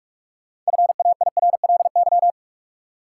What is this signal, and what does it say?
Morse code, heard as a keyed tone, about 42 wpm, 700 Hz. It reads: FAIRLY
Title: Sentences from Top 1100 Words - New Word Drill - 45wpm